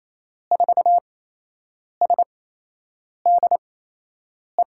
Morse code, heard as a keyed tone, about 28 wpm, spaced as for 8 wpm, 700 Hz. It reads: 4SDE
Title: USA Call Signs - 8x Character Spacing - 28wpm